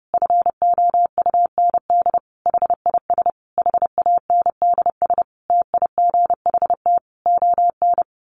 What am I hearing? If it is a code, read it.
Morse code, heard as a keyed tone, about 28 wpm, 700 Hz. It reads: FOUND HIS HANDS TIGHT ON
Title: A Sound Of Thunder 30wpm